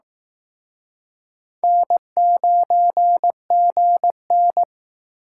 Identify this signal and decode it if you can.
Morse code, heard as a keyed tone, about 19 wpm, 700 Hz. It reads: N9GN
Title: Speed Racing - USA 1x2 Call Signs 15wpm